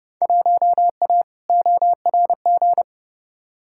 Morse code, heard as a keyed tone, about 30 wpm, 700 Hz. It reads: 1A ORG
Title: Field Day Exchanges - 30wpm